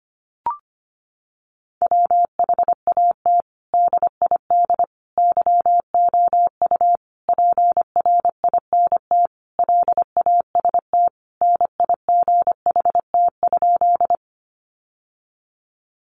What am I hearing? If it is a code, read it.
Morse code, heard as a keyed tone, about 25 wpm, 700 Hz. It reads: WHAT DID YOU PRINT LAST NIGHT?
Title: Sentences from Top 1400 Words 25wpm